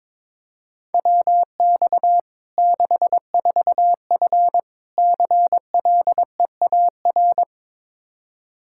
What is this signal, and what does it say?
Morse code, heard as a keyed tone, about 22 wpm, 700 Hz. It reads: WX 64F CLEAR
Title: K1USN SST QSO Practice 22wpm